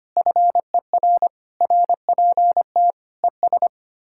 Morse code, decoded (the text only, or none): FER RPT ES